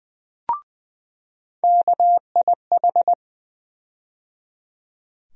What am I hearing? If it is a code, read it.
Morse code, heard as a keyed tone, about 20 wpm, 700 Hz. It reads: KIH
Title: USA Call Sign Postfixes - 3 Letters 20wpm